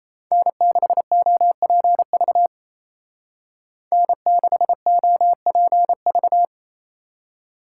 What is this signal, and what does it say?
Morse code, heard as a keyed tone, about 34 wpm, 700 Hz. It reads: N6OPV N6OPV
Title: Speed Racing - USA 1x3 Call Signs 28wpm